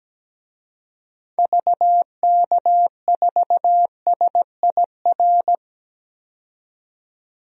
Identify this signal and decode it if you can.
Morse code, heard as a keyed tone, about 17 wpm, 700 Hz. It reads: VK4SIR
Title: DXCC-100 Call Signs - 6 Characters - 17wpm